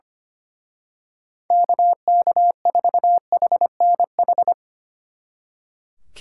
Morse code, heard as a keyed tone, about 25 wpm, 700 Hz. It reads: KK4HNH
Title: Speed Racing - USA 2x3 Call Signs 25wpm